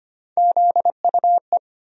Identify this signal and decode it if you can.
Morse code, heard as a keyed tone, about 25 wpm, 700 Hz. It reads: ZUE
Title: USA Call Sign Postfixes - 3 Letters 25wpm